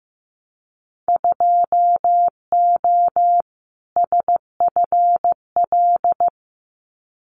Morse code, heard as a keyed tone, about 15 wpm, 700 Hz. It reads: 2O SFL
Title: Winter Field Day Exchanges 15wpm